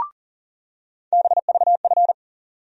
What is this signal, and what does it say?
Morse code, heard as a keyed tone, about 40 wpm, 700 Hz. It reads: BVF
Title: USA Call Sign Postfixes - 3 Letters 40wpm